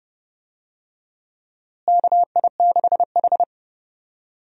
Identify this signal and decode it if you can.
Morse code, heard as a keyed tone, about 30 wpm, 700 Hz. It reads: KI6H
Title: USA Call Signs - 4 Characters - Repeated 3x - 30wpm